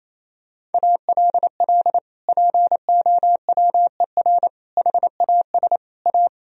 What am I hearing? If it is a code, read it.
Morse code, heard as a keyed tone, about 28 wpm, 700 Hz. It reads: ALL POWER HAS A
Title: Sentences from Top 1000 Words - Review Part 2 - 28wpm